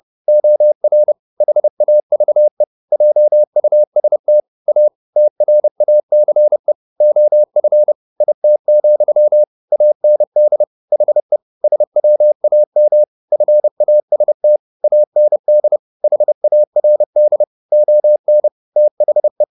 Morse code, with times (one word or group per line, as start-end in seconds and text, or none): 0.3-1.1 OR
1.4-2.6 HAVE
2.9-4.4 JUST
4.7-4.9 A
5.2-6.7 TRACE
7.0-7.9 OF
8.2-9.4 IT,
9.7-10.6 AND
10.9-11.4 HE
11.6-13.0 SWAM
13.3-14.6 FAST
14.8-15.8 AND
16.0-17.4 HARD
17.7-18.5 ON
18.8-19.4 THE